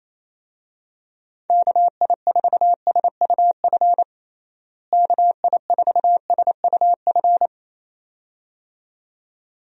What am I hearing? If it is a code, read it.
Morse code, heard as a keyed tone, about 28 wpm, 700 Hz. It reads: KI4SUF KI4SUF
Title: USA Call Signs - 6 Characters - Repeated 2x - 28wpm